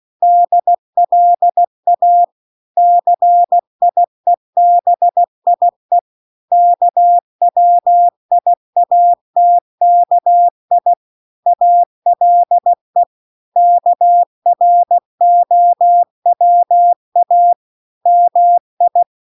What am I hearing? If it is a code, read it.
Morse code, heard as a keyed tone, about 16 wpm, 700 Hz. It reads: DLA CIEBIE KWIATKI ALE KROWA MI